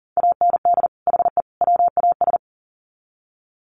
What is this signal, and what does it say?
Morse code, heard as a keyed tone, about 40 wpm, 700 Hz. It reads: AND HE WAS
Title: Sets of 3 Words 40wpm